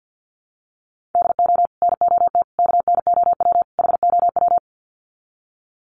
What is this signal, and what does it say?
Morse code, heard as a keyed tone, about 50 wpm, 700 Hz. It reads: DO NOT KNOW HOW
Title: Sets of 4 Words 50wpm